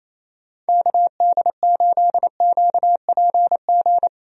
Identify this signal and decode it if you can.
Morse code, heard as a keyed tone, about 27 wpm, 700 Hz. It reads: KD8QPG
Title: USA Call Signs - 6 Characters - Repeated 3x - 28wpm